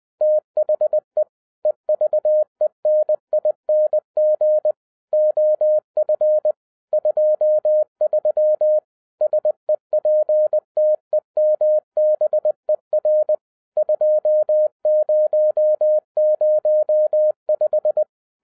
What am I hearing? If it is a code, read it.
Morse code, heard as a keyed tone, about 20 wpm, 610 Hz. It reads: THE EVENING OF 23 SEPTEMBER 2005